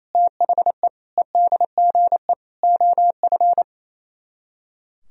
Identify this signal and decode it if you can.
Morse code, heard as a keyed tone, about 37 wpm, 700 Hz. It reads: THE EDGE OF